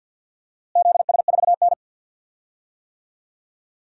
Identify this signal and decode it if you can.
Morse code, heard as a keyed tone, about 53 wpm, 700 Hz. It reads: ZS4N
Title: DXCC-100 Call Signs - 4 Characters - Repeated 3x - 50wpm